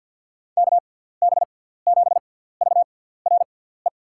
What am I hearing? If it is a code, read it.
Morse code, heard as a keyed tone, about 53 wpm, spaced as for 17 wpm, 700 Hz. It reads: KB7VRE
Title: USA Call Signs - 6 Characters - 6x Character Spacing - 50wpm